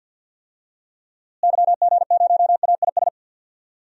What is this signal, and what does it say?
Morse code, heard as a keyed tone, about 50 wpm, 700 Hz. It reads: YG0AIS
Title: DXCC-100 Call Signs - 6 Characters - Repeated 3x - 50wpm